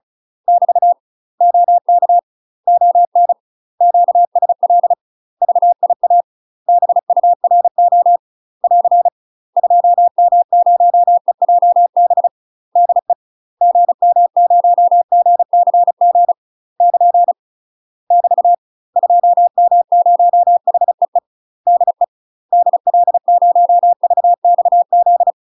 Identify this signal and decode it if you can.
Morse code, heard as a keyed tone, about 36 wpm, 700 Hz. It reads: <BT> OK ON QSL VIA BURO <AR> 2M0EJB DE GM0GCG <KN> <BT> 2M0HEE DE DL0VXZ
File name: UK MASTER QSOs TRAINING / 35WPM / UK MASTER QSOs 0010